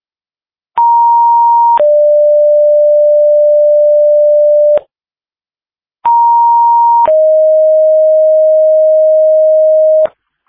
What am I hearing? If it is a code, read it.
Two-tone page, heard as a one-second tone, then a three-second tone, about 940 then 590 Hz, about 940 then 620 Hz. It is heard from 0.8 to 10.1 s.